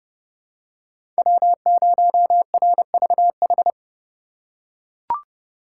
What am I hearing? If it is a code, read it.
Morse code, heard as a keyed tone, about 30 wpm, 700 Hz. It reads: W0RVH
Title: USA Call Signs - 5 Characters - 4x Character Spacing - 30wpm